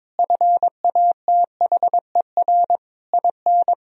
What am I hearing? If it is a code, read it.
Morse code, heard as a keyed tone, about 22 wpm, 700 Hz. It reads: FATHER IN
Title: Sentences from Top 1000 Words - Review Part 4 - 22wpm